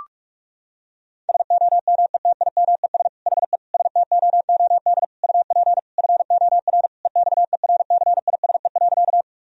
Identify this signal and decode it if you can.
Morse code, heard as a keyed tone, about 46 wpm, 700 Hz. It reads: SOMETIMES HE STOOD UP FOR EXERCISE.
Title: Sentences from Top 1000 Words - Review Part 4 - 45wpm